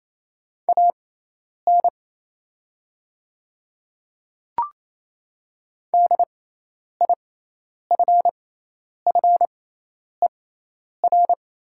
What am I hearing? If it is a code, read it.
Morse code, heard as a keyed tone, about 28 wpm, spaced as for 10 wpm, 700 Hz. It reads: AN DIFFER
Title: 6x Character Spacing - Top 200 Words - 28wpm